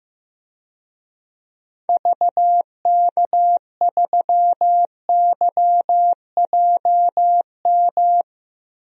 Morse code, heard as a keyed tone, about 16 wpm, 700 Hz. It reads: VK3YJM
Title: DXCC-100 Call Signs - 6 Characters - Repeated 2x - 15wpm